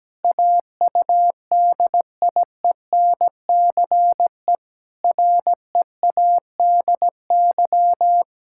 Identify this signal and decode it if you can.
Morse code, heard as a keyed tone, about 17 wpm, 700 Hz. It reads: AUDIENCE READY
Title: Sentences from Top 1000 Words - Review Part 5 - 17wpm